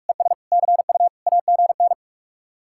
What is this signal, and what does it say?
Morse code, heard as a keyed tone, about 50 wpm, 700 Hz. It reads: ES CU AGN